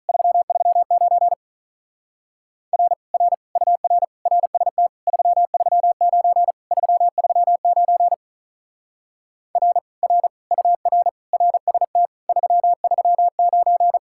Morse code, heard as a keyed tone, about 45 wpm, 700 Hz. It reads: 339 R R UR RST 339 339 R R UR RST 339